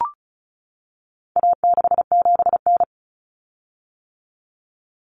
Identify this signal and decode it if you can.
Morse code, heard as a keyed tone, about 34 wpm, 700 Hz. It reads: A67N
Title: Common Cut Numbers - 4-Digit 35wpm